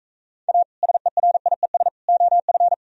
Morse code, heard as a keyed tone, about 42 wpm, 700 Hz. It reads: A SERIES OF